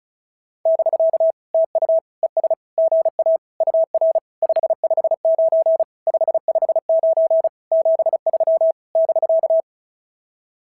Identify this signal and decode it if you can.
Morse code, heard as a keyed tone, about 33 wpm, 650 Hz. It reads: <BK> TU ES GA UR 559 559 73 <BK>
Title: SOTA QSO Practice 35wpm